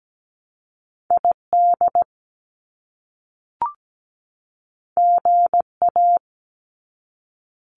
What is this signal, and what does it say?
Morse code, heard as a keyed tone, about 17 wpm, 700 Hz. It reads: ID GA
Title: US States 17wpm